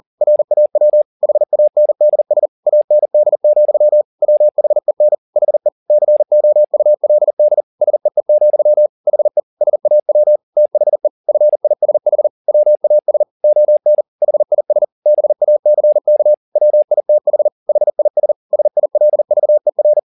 Morse code, heard as a keyed tone, about 40 wpm, 600 Hz. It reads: RAW HANDS AND, WHEN HE COULD SEE, HE SAW THE FISH WAS ON HIS BACK WITH HIS SILVER
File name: the old man and the sea 2 40wpm